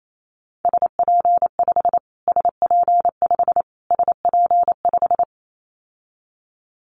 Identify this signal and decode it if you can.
Morse code, heard as a keyed tone, about 28 wpm, 700 Hz. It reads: SP5 SP5 SP5
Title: Callsign Prefixes for Top 25 DXCC - Repeated 3x - 28wpm